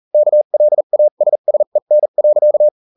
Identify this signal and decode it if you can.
Morse code, heard as a keyed tone, about 40 wpm, spaced as for 34 wpm, 600 Hz. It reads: KLASSEN.